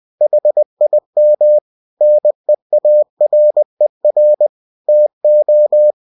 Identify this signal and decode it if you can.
Morse code, heard as a keyed tone, about 20 wpm, 600 Hz. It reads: HIM NEARER TO
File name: cw-books / winnie-the-pooh / winnie-20wpm-page-0007